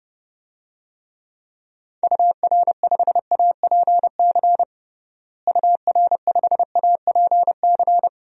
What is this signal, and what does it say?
Morse code, heard as a keyed tone, about 30 wpm, 700 Hz. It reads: UR5APC UR5APC
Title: DXCC-100 Call Signs - 6 Characters - Repeated 2x - 30wpm